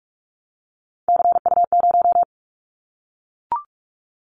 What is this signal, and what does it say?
Morse code, heard as a keyed tone, about 44 wpm, 700 Hz. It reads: CU0
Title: Callsign Prefixes for Top 100 DXCC - 45wpm